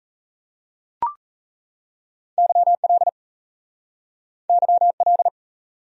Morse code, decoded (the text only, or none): YL YL